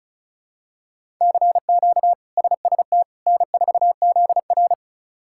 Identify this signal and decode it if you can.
Morse code, heard as a keyed tone, about 36 wpm, 700 Hz. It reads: CQ SST N4ZR
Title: K1USN SST QSO Practice - 35wpm